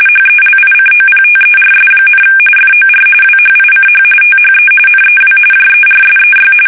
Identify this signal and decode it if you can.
RTTY, about 45.45 baud. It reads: HELICOPTERS, FOR WHICH THE NEW JERSEY
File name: fsk850 / battleshipnj